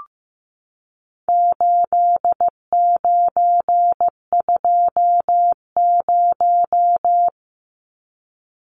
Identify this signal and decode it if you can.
Morse code, heard as a keyed tone, about 15 wpm, 700 Hz. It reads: 8920
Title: Numbers - 4-Digit 15wpm